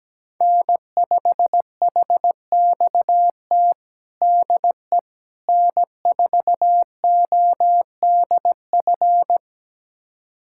Morse code, heard as a keyed tone, about 17 wpm, 700 Hz. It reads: N5HXT DE N4ODF
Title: USA Callsign DE Callsign - 5 Characters - 17wpm